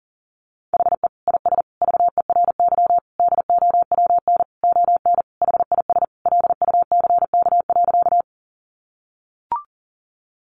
Morse code, heard as a keyed tone, about 40 wpm, 700 Hz. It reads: HE IS VERY DOWN ON HIS LUCK.
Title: Sentences from Top 1500 Words 40wpm